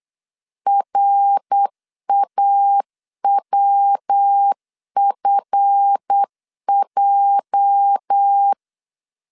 Morse code, heard as a keyed tone, about 8 wpm, 800 Hz. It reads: RAWFJ
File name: CW / DET / 1 9 1500C S C 8wpm 5653